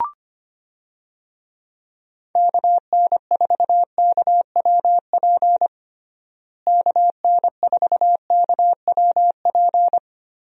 Morse code, heard as a keyed tone, about 25 wpm, 700 Hz. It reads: KN4KWP KN4KWP
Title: USA Call Signs - 6 Characters - Repeated 2x - 25wpm